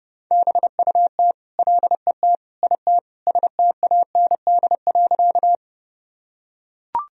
Morse code, heard as a keyed tone, about 30 wpm, 700 Hz. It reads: BUT LET IT STAND.
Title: Sentences from Top 1000 Words - Review Part 1 - 30wpm